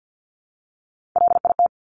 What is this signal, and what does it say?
Morse code, heard as a keyed tone, about 50 wpm, 700 Hz. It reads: LIT
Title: Top Three Letter Combinations in Words 50wpm